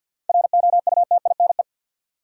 Morse code, heard as a keyed tone, about 48 wpm, 700 Hz. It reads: ROUTINE
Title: Seven Letter Words - 50wpm